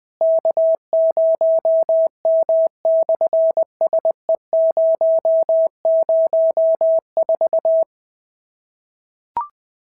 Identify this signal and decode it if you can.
Morse code, heard as a keyed tone, about 20 wpm, 650 Hz. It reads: K0M/SE004
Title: SOTA QSO Practice 20wpm